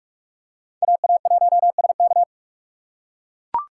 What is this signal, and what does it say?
Morse code, heard as a keyed tone, about 42 wpm, 700 Hz. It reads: AA1SK